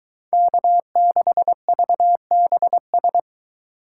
Morse code, heard as a keyed tone, about 22 wpm, 700 Hz. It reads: K6VBS